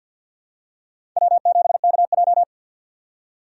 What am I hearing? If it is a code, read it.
Morse code, heard as a keyed tone, about 52 wpm, 700 Hz. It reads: W7KJ
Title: USA Call Signs - 4 Characters - 3x Character Spacing - 50wpm